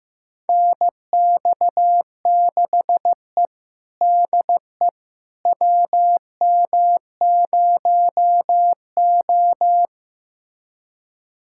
Morse code, heard as a keyed tone, about 15 wpm, 700 Hz. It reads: NX6E DE WM0O
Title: USA Callsign DE Callsign - 4 Characters - 15wpm